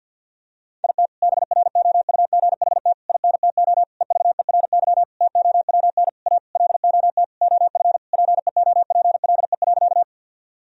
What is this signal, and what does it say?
Morse code, heard as a keyed tone, about 50 wpm, 700 Hz. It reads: IT BROUGHT INTO EVERY TOWN A LOT OF PEOPLE.
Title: Sentences from Top 1000 Words - Review Part 3 - 50wpm